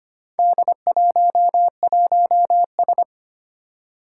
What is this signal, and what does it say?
Morse code, heard as a keyed tone, about 25 wpm, 700 Hz. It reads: D11S
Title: Four Challenging Letter-Number Combinations 25wpm